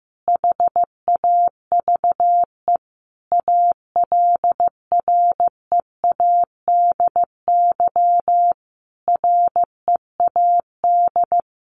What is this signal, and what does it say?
Morse code, heard as a keyed tone, about 15 wpm, 700 Hz. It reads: HAVE ALREADY READ
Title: Sentences from Top 700 Words 15wpm